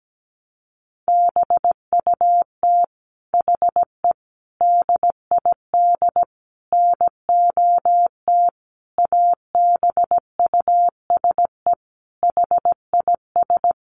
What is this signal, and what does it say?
Morse code, heard as a keyed tone, about 17 wpm, 700 Hz. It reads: BUT HE DID NOT ABUSE HIS
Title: Sentences from Top 1300 Words 17wpm